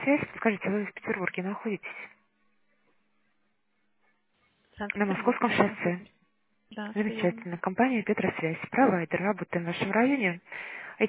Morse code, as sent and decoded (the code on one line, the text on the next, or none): none